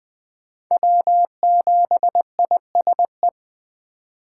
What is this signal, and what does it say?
Morse code, heard as a keyed tone, about 21 wpm, 700 Hz. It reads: W7ISE
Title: Speed Racing - USA 1x3 Call Signs 17wpm